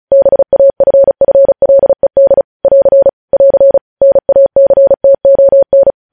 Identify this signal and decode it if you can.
Morse code, heard as a keyed tone, about 36 wpm, 550 Hz. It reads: BAFFLED <AR> <AR> NACTON
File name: podcast / News headlines-1x-35x35